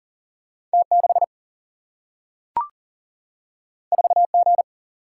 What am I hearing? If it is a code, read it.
Morse code, heard as a keyed tone, about 40 wpm, 700 Hz. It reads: T6 4G
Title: Single-Letter Single-Number 40wpm